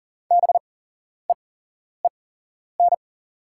Morse code, heard as a keyed tone, about 41 wpm, spaced as for 11 wpm, 700 Hz. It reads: BEEN
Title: 8x Character Spacing - Top 300 Words - 40wpm